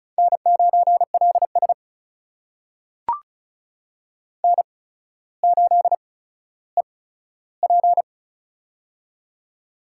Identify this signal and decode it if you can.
Morse code, heard as a keyed tone, about 36 wpm, spaced as for 15 wpm, 700 Hz. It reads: N9LS N8EP
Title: USA Call Signs - 8x Character Spacing - 35wpm